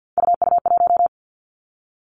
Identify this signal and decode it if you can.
Morse code, heard as a keyed tone, about 50 wpm, 700 Hz. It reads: UU1
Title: Three Challenging Letter-Number Combinations 50wpm